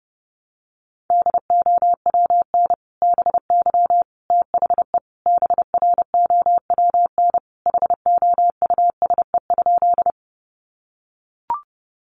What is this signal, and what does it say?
Morse code, heard as a keyed tone, about 30 wpm, 700 Hz. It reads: DOWN BY THE BROWN HOUSE?
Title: Sentences from Top 1200 Words 30wpm